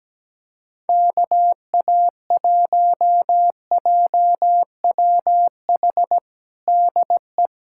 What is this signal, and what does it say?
Morse code, heard as a keyed tone, about 17 wpm, 700 Hz. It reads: KA1JWH DE
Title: USA Callsign DE Callsign - 6 Characters - 17wpm